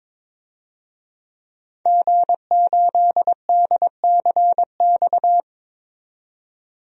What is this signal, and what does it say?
Morse code, heard as a keyed tone, about 22 wpm, 700 Hz. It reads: G8DCX
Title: DXCC-100 Call Signs - 5 Characters - 22wpm